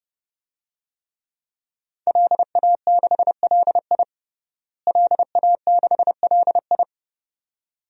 Morse code, heard as a keyed tone, about 30 wpm, 700 Hz. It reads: LA6LI LA6LI
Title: DXCC-100 Call Signs - 5 Characters - Repeated 2x - 30wpm